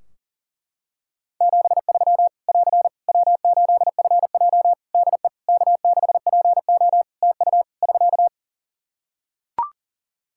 Morse code, heard as a keyed tone, about 43 wpm, 700 Hz. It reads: 73 <AR> W8FJ DE K6PO TU <SK>
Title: Practice QSO - Exchange 4 - Ending - 40wpm